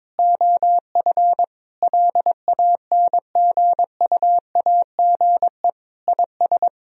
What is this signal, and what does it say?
Morse code, heard as a keyed tone, about 22 wpm, 700 Hz. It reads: OF LANGUAGE IS
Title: Sentences from Top 1000 Words - Review Part 3 - 22wpm